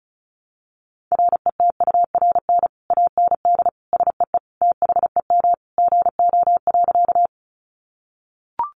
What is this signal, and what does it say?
Morse code, heard as a keyed tone, about 34 wpm, 700 Hz. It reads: RETURN AND SEE THEM GO.
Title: Sentences from Top 800 Words 35wpm